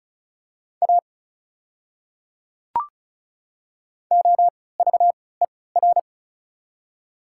Morse code, heard as a keyed tone, about 36 wpm, spaced as for 20 wpm, 690 Hz. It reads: A OVER
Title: 3x Character Spacing - Top 100 Words - 35wpm